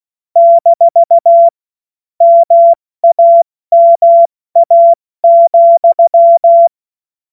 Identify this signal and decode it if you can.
Morse code, heard as a keyed tone, about 16 wpm, 680 Hz. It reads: - MAMA,
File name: Morsekurs / Witze / Witz 21. September 2021, 70 F80